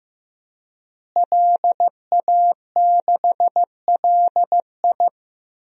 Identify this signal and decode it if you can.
Morse code, heard as a keyed tone, about 15 wpm, 700 Hz. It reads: LA6LI
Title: DXCC-100 Call Signs - 5 Characters - Repeated 2x - 15wpm